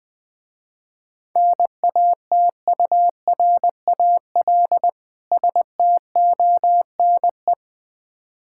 Morse code, heard as a keyed tone, about 20 wpm, 700 Hz. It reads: NATURAL STONE
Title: Difficult Words - Speed-Racing with Hints - 20wpm